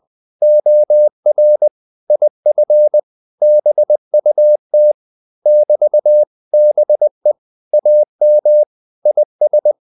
Morse code, heard as a keyed tone, about 20 wpm, 600 Hz. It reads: OR IF BUT <BT>BE AM IS
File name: CW / Practice / 500MostCommonEnglishWords20